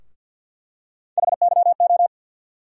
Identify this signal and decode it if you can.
Morse code, heard as a keyed tone, about 51 wpm, 700 Hz. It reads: HYO
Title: USA Call Sign Postfixes - 3 Letters 50wpm